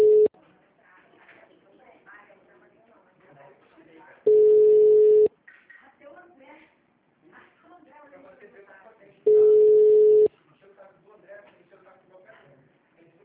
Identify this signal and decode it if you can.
Ringback tone (Europe, 425 Hz), with speech.